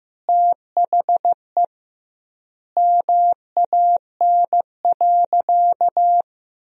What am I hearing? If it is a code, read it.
Morse code, heard as a keyed tone, about 16 wpm, 700 Hz. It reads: THE MAN.
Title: 2x Word-Spacing - Sentences from Top 400 Words - 15wpm